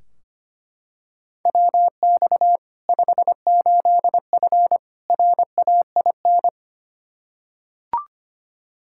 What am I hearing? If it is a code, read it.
Morse code, heard as a keyed tone, about 25 wpm, 700 Hz. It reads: WX 58F RAIN
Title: Practice QSO - Exchange 2 - Info - 25wpm